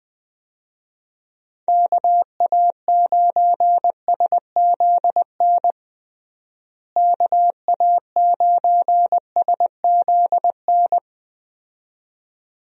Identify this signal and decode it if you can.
Morse code, heard as a keyed tone, about 21 wpm, 700 Hz. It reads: KA9SZN KA9SZN
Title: USA Call Signs - 6 Characters - Repeated 2x - 20wpm